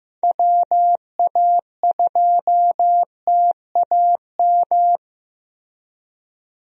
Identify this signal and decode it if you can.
Morse code, heard as a keyed tone, about 15 wpm, 700 Hz. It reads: WA2TAM
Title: USA Call Signs - 6 Characters - Repeated 3x - 15wpm